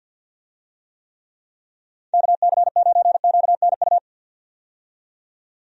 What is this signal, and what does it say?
Morse code, heard as a keyed tone, about 52 wpm, 700 Hz. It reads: KC9QNU
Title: USA Call Signs - 6 Characters - 50wpm